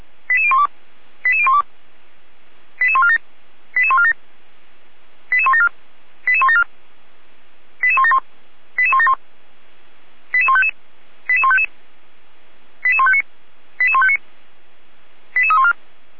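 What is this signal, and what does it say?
Five-tone selective call (ZVEI), heard as quick runs of five tones, tones of about 80 ms. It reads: 88012 88012 88147 88147 88174 88174 88171 88171 88155 88155 88159 88159 88314
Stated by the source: ZVEI five-tone call received on SDR, from feuerwehr-orlamuende.org fire brigade audio server.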